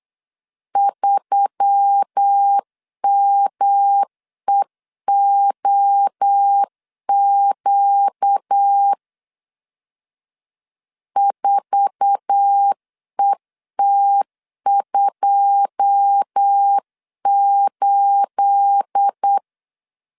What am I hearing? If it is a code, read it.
Morse code, heard as a keyed tone, about 8 wpm, 800 Hz. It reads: 3MEOQ 4ET28
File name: LEZIONI / 9 28 8wpm 3512